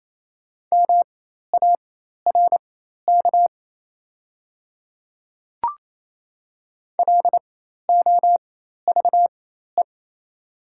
Morse code, heard as a keyed tone, about 27 wpm, spaced as for 13 wpm, 700 Hz. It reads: MARK LOVE
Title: 4x Character Spacing - Top 500 Words - 28wpm